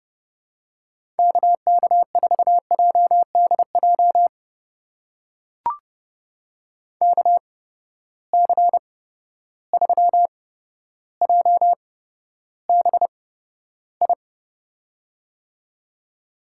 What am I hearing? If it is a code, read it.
Morse code, heard as a keyed tone, about 30 wpm, spaced as for 13 wpm, 700 Hz. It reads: KK4JDJ KC3JBI
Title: USA Call Signs - 6 Characters - 8x Character Spacing - 30wpm